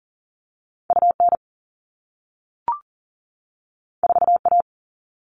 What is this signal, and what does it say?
Morse code, heard as a keyed tone, about 42 wpm, 700 Hz. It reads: UN 4A
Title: All Cut Numbers - 2-Digit 40wpm